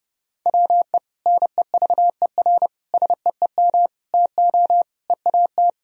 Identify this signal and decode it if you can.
Morse code, heard as a keyed tone, about 30 wpm, 700 Hz. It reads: WE NEVER SEEM TO EAT
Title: Sentences from Top 1000 Words - Review Part 1 - 30wpm